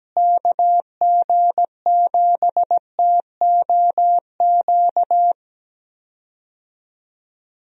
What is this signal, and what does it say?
Morse code, heard as a keyed tone, about 17 wpm, 700 Hz. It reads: KG7TOQ
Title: USA Call Signs - 6 Characters - Repeated 3x - 17wpm